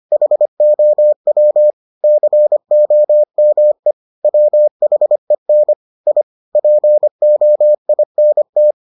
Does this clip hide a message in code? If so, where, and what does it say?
Yes: Morse code from 0.1 to 8.7 s, heard as a keyed tone, about 25 wpm, 600 Hz. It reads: HOW COME WHEN I POINT